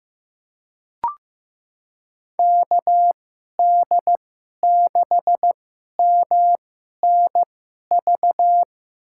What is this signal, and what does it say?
Morse code, heard as a keyed tone, about 16 wpm, spaced as for 11 wpm, 700 Hz. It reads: KD6MNV
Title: USA Call Signs - 6 Characters - 2x Character Spacing - 15wpm